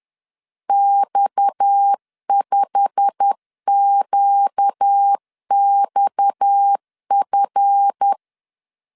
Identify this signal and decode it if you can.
Morse code, heard as a keyed tone, about 10 wpm, 800 Hz. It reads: X5QXF
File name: CW / DET / 1 9 1500C S B 10wpm 4439